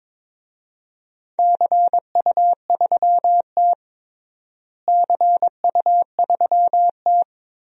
Morse code, heard as a keyed tone, about 22 wpm, 700 Hz. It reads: CU3T CU3T
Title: DXCC-100 Call Signs - 4 Characters - Repeated 3x - 22wpm